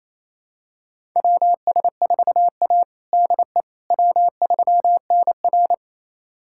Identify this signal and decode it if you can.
Morse code, heard as a keyed tone, about 28 wpm, 700 Hz. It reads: WS4A DE W3NR
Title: USA Callsign DE Callsign - 4 Characters - 28wpm